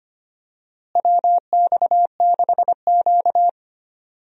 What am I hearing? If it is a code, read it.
Morse code, heard as a keyed tone, about 25 wpm, 700 Hz. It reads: WX6Q
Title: USA Call Signs - 4 Characters - Repeated 2x - 25wpm